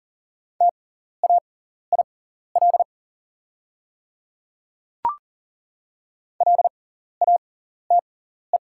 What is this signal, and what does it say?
Morse code, heard as a keyed tone, about 40 wpm, spaced as for 14 wpm, 700 Hz. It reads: TAIL LATE
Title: 6x Character Spacing - Top 500 Words - 40wpm